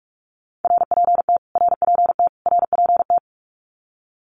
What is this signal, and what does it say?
Morse code, heard as a keyed tone, about 44 wpm, 700 Hz. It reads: RPT RPT RPT
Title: Abbreviations Repeated 3x - 45wpm